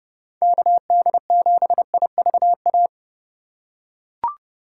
Morse code, heard as a keyed tone, about 30 wpm, 700 Hz. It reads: KD7IVA
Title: USA Call Signs - 6 Characters - 3x Character Spacing - 30wpm